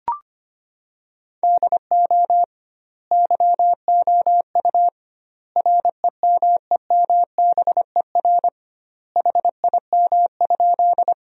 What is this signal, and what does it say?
Morse code, heard as a keyed tone, about 25 wpm, 700 Hz. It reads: DO YOU REMEMBER HIM?
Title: 2x Word-Spacing - Sentences from Top 400 Words - 25wpm